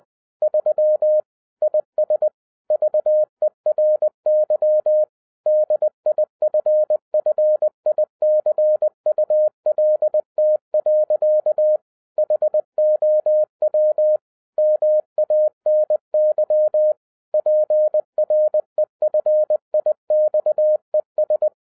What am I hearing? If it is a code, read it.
Morse code, heard as a keyed tone, about 20 wpm, 610 Hz. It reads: 3 IS VERY DIFFICULT. HOW MANY PREFIXES